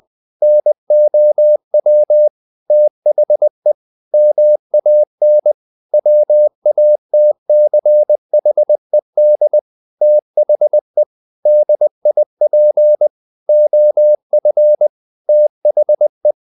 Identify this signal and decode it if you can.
Morse code, heard as a keyed tone, about 20 wpm, 600 Hz. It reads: NOW THE MAN WATCHED THE DIP OF THE
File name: the old man and the sea / the old man and the sea part 1 20wpm